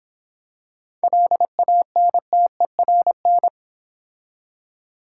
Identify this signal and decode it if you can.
Morse code, heard as a keyed tone, about 26 wpm, 700 Hz. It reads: LANTERN